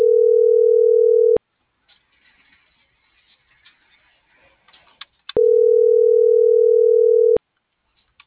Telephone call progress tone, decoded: ringback tone

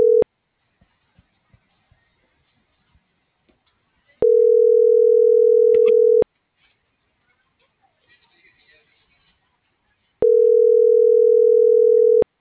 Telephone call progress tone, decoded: ringback tone

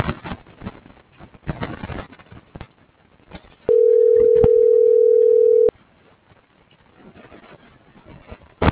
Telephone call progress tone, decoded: ringback tone

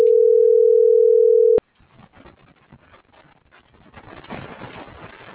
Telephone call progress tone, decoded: ringback tone

